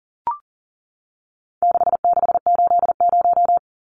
Morse code, heard as a keyed tone, about 40 wpm, 700 Hz. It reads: 6680
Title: Numbers - 4-Digit 40wpm